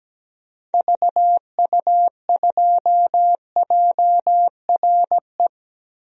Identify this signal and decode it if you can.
Morse code, heard as a keyed tone, about 17 wpm, 700 Hz. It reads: VU2JRE
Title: DXCC-100 Call Signs - 6 Characters - Repeated 3x - 17wpm